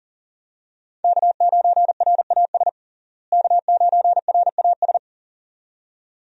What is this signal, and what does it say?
Morse code, heard as a keyed tone, about 40 wpm, 700 Hz. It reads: K9RAS K9RAS
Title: USA Call Signs - 5 Characters - Repeated 2x - 40wpm